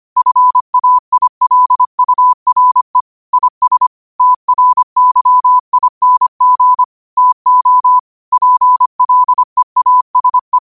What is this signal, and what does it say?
Morse code, heard as a keyed tone, about 25 wpm, 1000 Hz. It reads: FAILURE IS TRYING TO PLEASE
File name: 25wpm